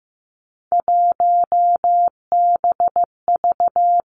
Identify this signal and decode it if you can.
Morse code, heard as a keyed tone, about 15 wpm, 700 Hz. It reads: 1BV
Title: Three Challenging Letter-Number Combinations 15wpm